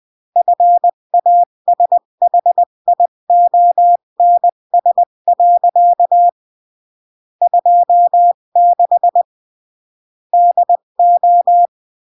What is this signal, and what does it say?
Morse code, heard as a keyed tone, about 20 wpm, 700 Hz. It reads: FASHIONS. 26 DO
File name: CW / jodorowsky / 82 commandments 20-15